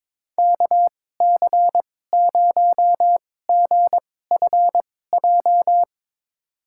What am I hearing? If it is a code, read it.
Morse code, heard as a keyed tone, about 22 wpm, spaced as for 16 wpm, 700 Hz. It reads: KC0GFJ
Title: USA Call Signs - 6 Characters - 2x Character Spacing - 22wpm